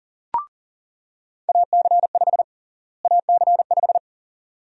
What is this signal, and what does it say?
Morse code, heard as a keyed tone, about 40 wpm, 700 Hz. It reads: AC5 AC5
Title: USA Call Sign Prefixes - Repeated 2x - 40wpm